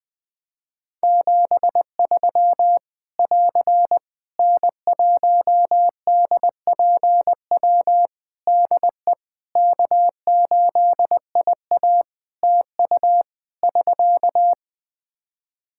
Morse code, heard as a keyed tone, about 20 wpm, 700 Hz. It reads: 73 <AR> N1DPW DE K8IA TU <SK>
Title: K1USN SST QSO Practice 20wpm